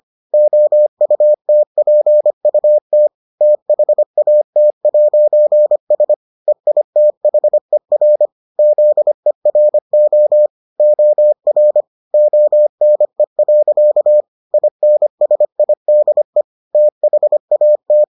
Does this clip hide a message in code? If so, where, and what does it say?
Yes: Morse code from 0.3 to 18.0 s, heard as a keyed tone, about 25 wpm, 600 Hz. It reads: OUTPUT THAT'S EITHER ZERO OR ONE. INSIDE THAT